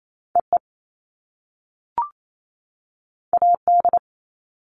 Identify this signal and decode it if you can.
Morse code, heard as a keyed tone, about 28 wpm, 700 Hz. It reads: EE AD
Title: All Cut Numbers - 2-Digit 28wpm